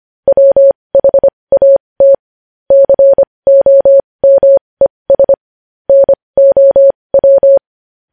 Morse code, heard as a keyed tone, about 25 wpm, spaced as for 20 wpm, 550 Hz. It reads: WHAT COMES NOW
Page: News Headlines in Morse Code at 20 WPM